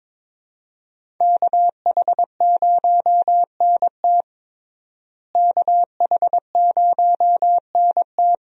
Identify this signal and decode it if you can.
Morse code, heard as a keyed tone, about 21 wpm, 700 Hz. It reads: KH0NT KH0NT
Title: USA Call Signs - 5 Characters - Repeated 2x - 22wpm